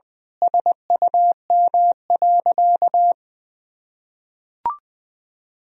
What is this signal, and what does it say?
Morse code, heard as a keyed tone, about 21 wpm, 700 Hz. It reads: SUM.
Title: Sentences from Top 100 Words and Abbreviations 20wpm